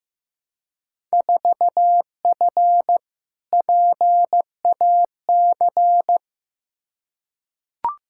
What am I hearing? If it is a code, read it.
Morse code, heard as a keyed tone, about 15 wpm, 700 Hz. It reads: 4F PAC
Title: Field Day Exchanges Encore - 15wpm